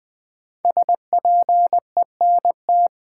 Morse code, heard as a keyed tone, about 20 wpm, 700 Hz. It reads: SPENT
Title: Build A Phrase - 5-Word Phrases - 20wpm